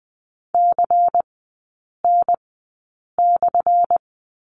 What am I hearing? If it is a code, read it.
Morse code, heard as a keyed tone, about 20 wpm, spaced as for 8 wpm, 700 Hz. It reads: CN/
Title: 178 Lesson 29e - Slash - Three Characters 20wpm